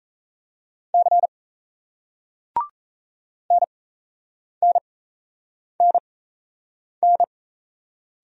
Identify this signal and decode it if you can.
Morse code, heard as a keyed tone, about 35 wpm, spaced as for 8 wpm, 700 Hz. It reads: C NNNN